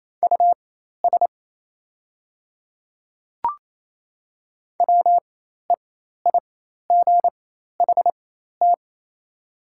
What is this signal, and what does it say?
Morse code, heard as a keyed tone, about 28 wpm, spaced as for 13 wpm, 700 Hz. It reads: US WEIGHT
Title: 4x Character Spacing - Top 500 Words - 28wpm